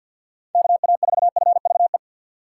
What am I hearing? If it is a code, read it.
Morse code, heard as a keyed tone, about 54 wpm, 700 Hz. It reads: K A 4 F V E
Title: USA Call Signs - 6 Characters - 8x Character Spacing - 50wpm